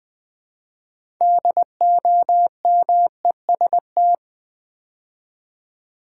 Morse code, heard as a keyed tone, about 20 wpm, 700 Hz. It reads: DOMEST
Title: Build A Word - 2 Letters At A Time - 8-Letter Words - 20wpm